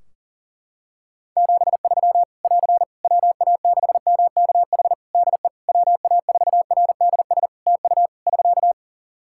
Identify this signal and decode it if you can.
Morse code, heard as a keyed tone, about 40 wpm, 700 Hz. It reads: 73 <AR> WA6MKH DE WA4RDS TU <SK>
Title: Practice QSO - Exchange 4 - Ending - 40wpm